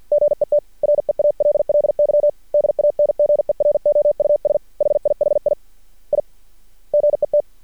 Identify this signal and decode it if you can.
Morse code, heard as a keyed tone, about 51 wpm, 590 Hz. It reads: GET REALLY DANGEROUS HIHI I GET
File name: archived / CRSnet